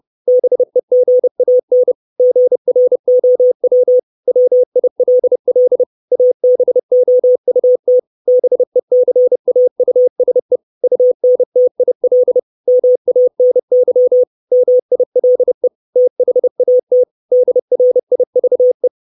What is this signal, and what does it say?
Morse code, heard as a keyed tone, about 30 wpm, 500 Hz. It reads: BEGAN GROW WILL ABOUT BECAUSE UNTIL MANY MILE THAT DRIVE
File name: 632 Words 30WPM